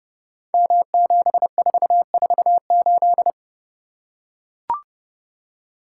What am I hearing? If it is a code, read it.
Morse code, heard as a keyed tone, about 31 wpm, 700 Hz. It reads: M7448
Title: Code Groups - 5 Letter:Numbers - 30wpm